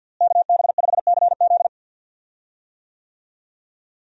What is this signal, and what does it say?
Morse code, heard as a keyed tone, about 52 wpm, 700 Hz. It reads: KB5CZ
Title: USA Call Signs - 5 Characters - Repeated 2x - 50wpm